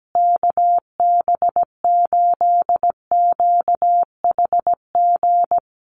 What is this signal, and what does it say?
Morse code, heard as a keyed tone, about 17 wpm, 700 Hz. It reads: KB8QHG
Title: USA Call Signs Repeated 2x 17wpm